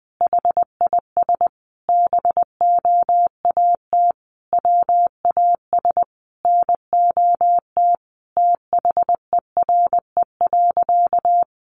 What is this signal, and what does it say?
Morse code, heard as a keyed tone, about 20 wpm, 700 Hz. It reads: HIS BOAT WAS NOT THERE.
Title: Sentences from Top 1400 Words 20wpm